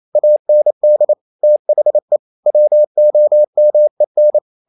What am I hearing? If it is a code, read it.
Morse code, heard as a keyed tone, about 28 wpm, 600 Hz. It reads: AND THE WOMEN
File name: Never 0027wpm